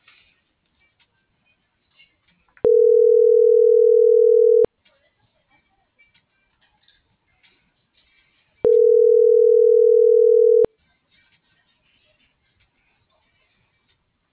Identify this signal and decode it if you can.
Ringback tone (North America), 440 + 480 Hz.